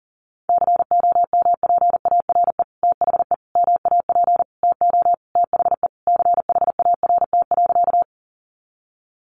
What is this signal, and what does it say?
Morse code, heard as a keyed tone, about 40 wpm, 700 Hz. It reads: COMPARE THE MAP TO THE CHART.
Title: Sentences from Top 1300 Words 40wpm